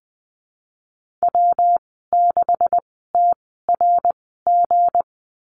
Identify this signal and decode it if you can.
Morse code, heard as a keyed tone, about 20 wpm, spaced as for 15 wpm, 700 Hz. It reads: W6TRG
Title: 179 Lesson 29g - Slash - Call Signs 20wpm